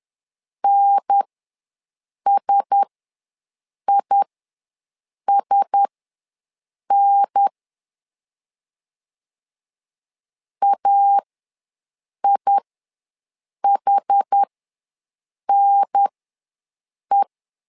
Morse code, heard as a keyed tone, about 10 wpm, spaced as for 6 wpm, 800 Hz. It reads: NSISN AIHNE
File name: LEZIONI / 4 Q1S ws 10wpm 2208